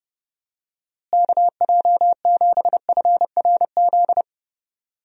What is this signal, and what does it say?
Morse code, heard as a keyed tone, about 30 wpm, 700 Hz. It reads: KJ7FRZ
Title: USA Call Signs - 6 Characters - Repeated 3x - 30wpm